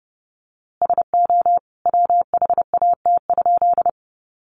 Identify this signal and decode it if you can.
Morse code, heard as a keyed tone, about 30 wpm, 700 Hz. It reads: SO WHAT?
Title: A Sound Of Thunder 30wpm